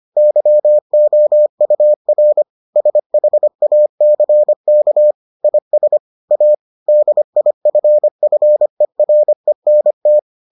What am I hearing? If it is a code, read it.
Morse code, heard as a keyed tone, about 25 wpm, 600 Hz. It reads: YOUR SHACK IS A DIFFERENT